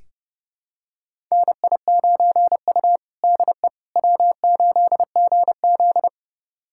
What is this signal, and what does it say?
Morse code, heard as a keyed tone, about 30 wpm, 700 Hz. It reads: NI9U DE W8GZ
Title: USA Callsign DE Callsign - 4 Characters - 30wpm